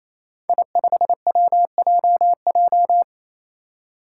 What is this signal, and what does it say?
Morse code, heard as a keyed tone, about 29 wpm, 700 Hz. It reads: I5WJJ
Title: Five Challenging Letter-Number Combinations 28wpm